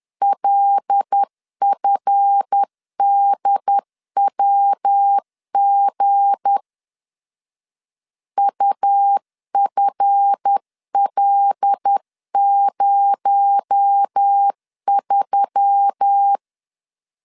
Morse code, heard as a keyed tone, about 10 wpm, 800 Hz. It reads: LFDWG UFL03
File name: LEZIONI / 7 GF9Z / 7 GF 10wpm 2514